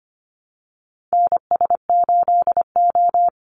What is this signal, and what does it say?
Morse code, heard as a keyed tone, about 25 wpm, 700 Hz. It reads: NS8O
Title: Call Signs Repeated 2x - 25wpm